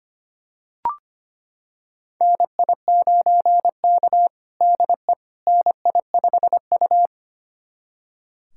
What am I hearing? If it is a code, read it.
Morse code, heard as a keyed tone, about 25 wpm, 700 Hz. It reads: NI9K DE NI5U
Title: USA Callsign DE Callsign - 4 Characters - 25wpm